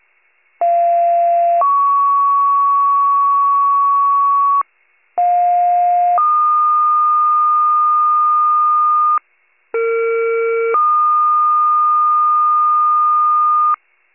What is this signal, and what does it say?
Two-tone page, heard as a one-second tone, then a three-second tone, about 690 then 1120 Hz, about 710 then 1220 Hz, about 460 then 1220 Hz. It is heard from 0.6 to 13.7 s.
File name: wcalert